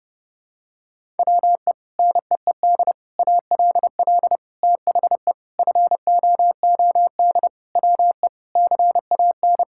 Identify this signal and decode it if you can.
Morse code, heard as a keyed tone, about 32 wpm, 700 Hz. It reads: WE NEED ALL THE FOOD WE CAN
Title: Sentences from Top 1000 Words - Review Part 5 - 30wpm